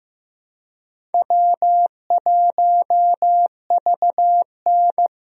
Morse code, heard as a keyed tone, about 16 wpm, 700 Hz. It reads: W1VN
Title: USA Call Signs - 4 Characters - 15wpm